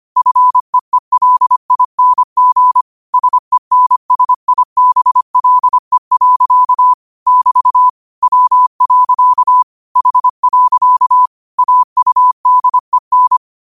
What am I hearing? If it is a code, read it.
Morse code, heard as a keyed tone, about 25 wpm, 1000 Hz. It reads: FEELING SENSIBLE. <BT> W. H. AUDEN
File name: qotdfiles / 25wpm